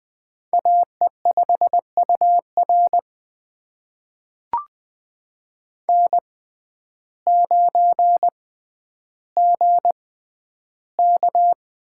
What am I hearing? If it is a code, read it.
Morse code, heard as a keyed tone, about 20 wpm, spaced as for 11 wpm, 700 Hz. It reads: AE5UR N9GK
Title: USA Call Signs - 5 Characters - 6x Character Spacing - 20wpm